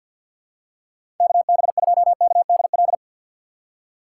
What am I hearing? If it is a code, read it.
Morse code, heard as a keyed tone, about 49 wpm, 700 Hz. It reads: KB2KDL
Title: USA Call Signs - 6 Characters - 6x Character Spacing - 50wpm